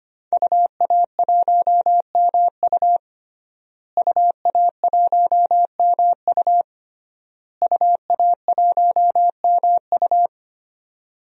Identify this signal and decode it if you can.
Morse code, heard as a keyed tone, about 25 wpm, 700 Hz. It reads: UA1MU UA1MU UA1MU
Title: DXCC-100 Call Signs - 5 Characters - Repeated 3x - 25wpm